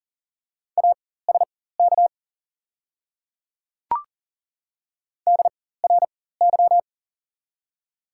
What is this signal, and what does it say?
Morse code, heard as a keyed tone, about 39 wpm, spaced as for 19 wpm, 700 Hz. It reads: ASK DRY
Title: 4x Character Spacing - Top 500 Words - 40wpm